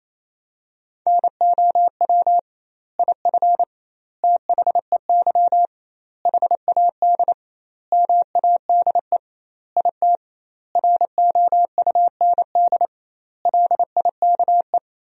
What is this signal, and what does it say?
Morse code, heard as a keyed tone, about 27 wpm, 700 Hz. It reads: NOW IF THEY HAD MADE IT ROUND LIKE
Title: 2x Word-Spacing - Sentences from Top 500 Words - 28wpm